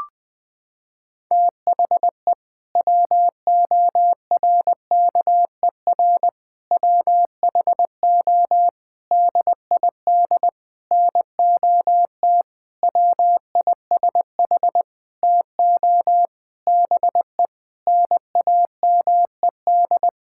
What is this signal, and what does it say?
Morse code, heard as a keyed tone, about 20 wpm, 700 Hz. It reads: THE WORKER WHO DID NOT WISH TO BE NAMED
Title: Sentences with 7-Word Phrases - 20wpm